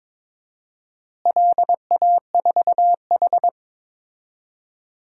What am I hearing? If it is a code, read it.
Morse code, heard as a keyed tone, about 22 wpm, 700 Hz. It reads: LA4H
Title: DXCC-100 Call Signs - 4 Characters - Repeated 2x - 22wpm